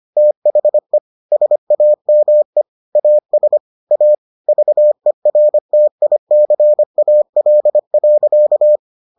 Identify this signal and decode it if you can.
Morse code, heard as a keyed tone, about 25 wpm, 600 Hz. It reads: THE SAME AS A VERTICAL.